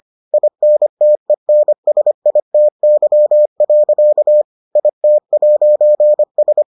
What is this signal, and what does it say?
Morse code, heard as a keyed tone, about 25 wpm, 600 Hz. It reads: INTENSITY. IT'S